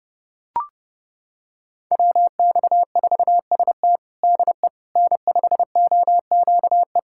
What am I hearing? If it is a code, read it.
Morse code, heard as a keyed tone, about 30 wpm, 700 Hz. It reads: WX4ST DE N5OQE
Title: USA Callsign DE Callsign 30wpm